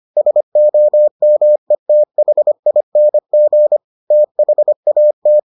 Morse code, heard as a keyed tone, about 25 wpm, 600 Hz. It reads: SOMETHING THAT